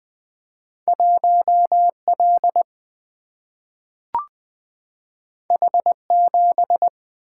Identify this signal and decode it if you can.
Morse code, heard as a keyed tone, about 20 wpm, 700 Hz. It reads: 1 L H 7
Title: Single-Letter Single-Number 20wpm